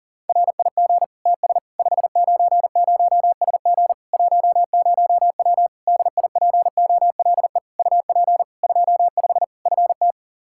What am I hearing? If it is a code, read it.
Morse code, heard as a keyed tone, about 40 wpm, 700 Hz. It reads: RIG TS 590SG 10W DIPOLE UP 25 FT
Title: K1USN SST QSO Practice 40wpm